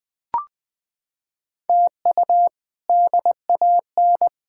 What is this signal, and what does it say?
Morse code, heard as a keyed tone, about 21 wpm, 700 Hz. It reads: TU DAN
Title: K1USN SST QSO Practice - 20wpm